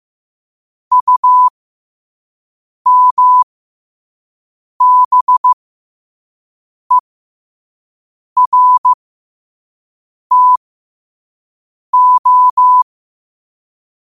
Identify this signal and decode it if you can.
Morse code, heard as a keyed tone, about 15 wpm, spaced as for 5 wpm, 1000 Hz. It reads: UMBERTO
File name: 5wpm